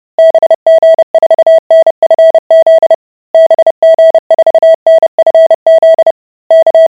Morse code, heard as a keyed tone, about 30 wpm, 650 Hz. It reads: BG4NFZ BG4NFZ K